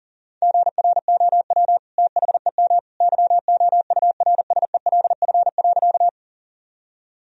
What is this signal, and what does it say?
Morse code, heard as a keyed tone, about 37 wpm, 700 Hz. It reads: GROW THEM YOURSELF.
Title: Sentences from Top 1000 Words - Review Part 1 - 40wpm